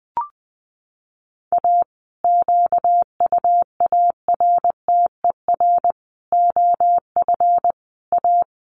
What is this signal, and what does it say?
Morse code, heard as a keyed tone, about 20 wpm, 700 Hz. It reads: A QUARTER OF A
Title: Sets of 5 Words Encore2 20wpm